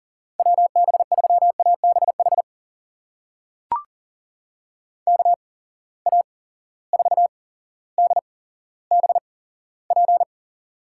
Morse code, heard as a keyed tone, about 40 wpm, spaced as for 17 wpm, 700 Hz. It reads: WB3ABH KA4DBP
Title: USA Call Signs - 6 Characters - 8x Character Spacing - 40wpm